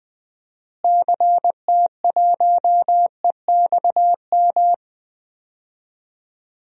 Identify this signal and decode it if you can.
Morse code, heard as a keyed tone, about 20 wpm, 700 Hz. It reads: CT1EXM
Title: DXCC-100 Call Signs - 6 Characters - Repeated 2x - 20wpm